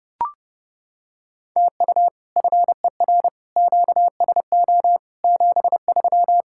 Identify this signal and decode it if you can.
Morse code, heard as a keyed tone, about 30 wpm, 700 Hz. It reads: TU FER QSO 73